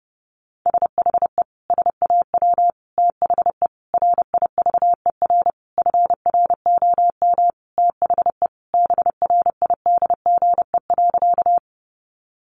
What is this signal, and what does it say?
Morse code, heard as a keyed tone, about 30 wpm, 700 Hz. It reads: SHE SAW THE RIVER FROM THE BRIDGE.
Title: Sentences from Top 1200 Words 30wpm